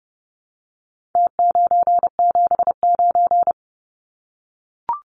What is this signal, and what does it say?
Morse code, heard as a keyed tone, about 30 wpm, 700 Hz. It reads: T979
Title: Numbers - Contest Style - 4-Digit 30wpm